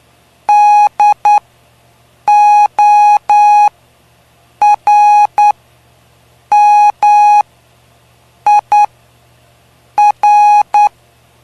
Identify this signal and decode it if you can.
Morse code, heard as a keyed tone, about 9 wpm, spaced as for 6 wpm, 830 Hz. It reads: DORMIR
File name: cw / py2lcd / 05-PY2LCD-Li��o 5